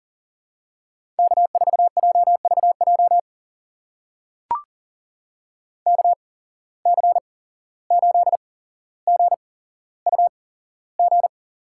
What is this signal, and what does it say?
Morse code, heard as a keyed tone, about 40 wpm, spaced as for 16 wpm, 700 Hz. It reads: K4JVJ KC8GUG
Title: USA Call Signs - 8x Character Spacing - 40wpm